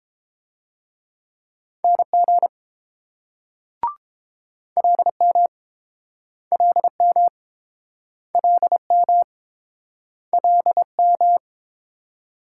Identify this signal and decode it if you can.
Morse code, heard as a keyed tone, about 28 wpm, 690 Hz. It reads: NG LM LM LM LM